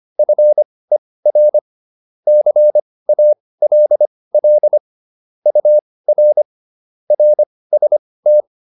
Morse code, heard as a keyed tone, about 25 wpm, spaced as for 18 wpm, 600 Hz. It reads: FER CALL UR RST
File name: QSO410 18